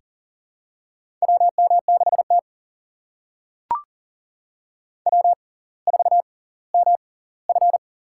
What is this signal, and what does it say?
Morse code, heard as a keyed tone, about 40 wpm, spaced as for 20 wpm, 700 Hz. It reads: WM6T W4MF
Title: USA Call Signs - 6x Character Spacing - 40wpm